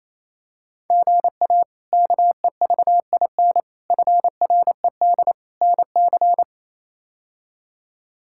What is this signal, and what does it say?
Morse code, heard as a keyed tone, about 28 wpm, 700 Hz. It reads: GA KEVIN FRED NC
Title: K1USN SST QSO Practice - 28wpm